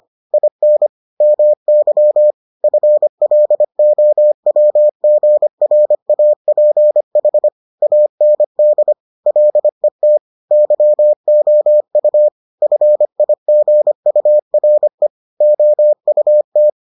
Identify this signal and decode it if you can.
Morse code, heard as a keyed tone, about 25 wpm, 600 Hz. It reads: IN MY FLOWGRAPH AND LET YOU FIGURE OUT